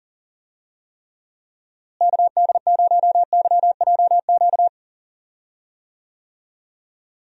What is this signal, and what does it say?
Morse code, heard as a keyed tone, about 38 wpm, 700 Hz. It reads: KD0YJQ
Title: USA Call Signs - 6 Characters - Repeated 3x - 40wpm